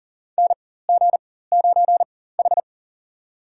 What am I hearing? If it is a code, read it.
Morse code, heard as a keyed tone, about 40 wpm, spaced as for 19 wpm, 700 Hz. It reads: NG9H
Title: USA Call Signs - 4 Characters - 4x Character Spacing - 40wpm